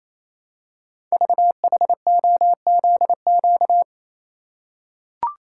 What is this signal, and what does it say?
Morse code, heard as a keyed tone, about 27 wpm, 700 Hz. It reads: VHOZQ
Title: Code Groups - 5 Letters - 28wpm